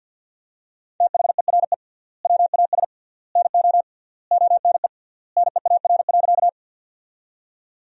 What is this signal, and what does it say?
Morse code, heard as a keyed tone, about 50 wpm, 700 Hz. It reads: THERE WAS NO ONE NEAR.
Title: 3x Word-Spacing - Sentences from Top 200 Words - 50wpm